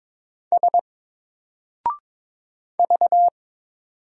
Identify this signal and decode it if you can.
Morse code, heard as a keyed tone, about 22 wpm, 700 Hz. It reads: S V
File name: Single-Letter-Number-22wpm